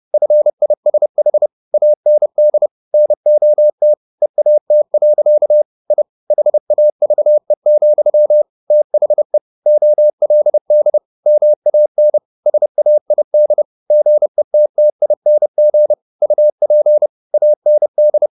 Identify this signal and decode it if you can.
Morse code, heard as a keyed tone, about 30 wpm, 600 Hz. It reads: FISH AND NOT EAT. I HAVE, THE OLD MAN SAID GETTING UP AND
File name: the old man and the sea / the old man and the sea 1 30wpm